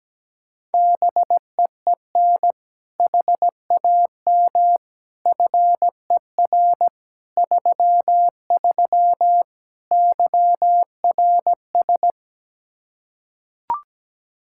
Morse code, heard as a keyed tone, about 17 wpm, 700 Hz. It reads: BEEN HAM FER 33 YRS
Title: Practice QSO - Exchange 3 - More Info - 17wpm